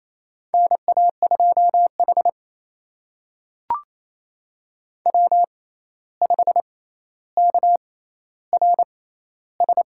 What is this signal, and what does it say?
Morse code, heard as a keyed tone, about 28 wpm, spaced as for 13 wpm, 700 Hz. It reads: NA2H W5KRS
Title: USA Call Signs - 6x Character Spacing - 28wpm